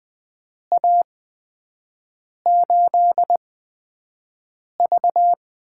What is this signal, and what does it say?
Morse code, heard as a keyed tone, about 20 wpm, spaced as for 5 wpm, 700 Hz. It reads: A8V
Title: USA Call Signs - 8x Character Spacing - 20wpm